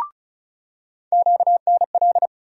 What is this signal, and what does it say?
Morse code, heard as a keyed tone, about 36 wpm, 700 Hz. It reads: QNL
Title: USA Call Sign Postfixes - 3 Letters 35wpm